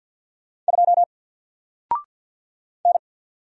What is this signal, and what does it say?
Morse code, heard as a keyed tone, about 49 wpm, 700 Hz. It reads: 2 N
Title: Single Letter-Number Alt 50wpm